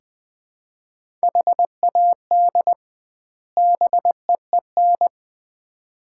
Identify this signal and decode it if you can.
Morse code, heard as a keyed tone, about 20 wpm, 700 Hz. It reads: HAD BEEN
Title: 2x Word-Spacing - Sets of 2 Words - 20wpm